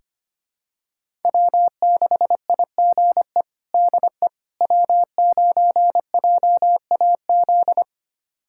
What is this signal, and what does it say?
Morse code, heard as a keyed tone, about 25 wpm, 700 Hz. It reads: W6IGE DE W9JAZ
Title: USA Callsign DE Callsign - 5 Characters - 25wpm